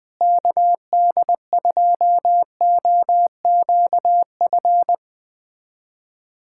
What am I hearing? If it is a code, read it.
Morse code, heard as a keyed tone, about 20 wpm, 700 Hz. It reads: KD2OQF
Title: USA Callsign DE Callsign - 6 Characters - 20wpm